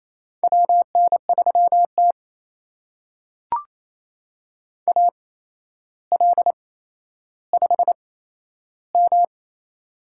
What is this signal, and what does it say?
Morse code, heard as a keyed tone, about 27 wpm, spaced as for 12 wpm, 700 Hz. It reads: WN3T AL5M
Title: USA Call Signs - 4 Characters - 8x Character Spacing - 28wpm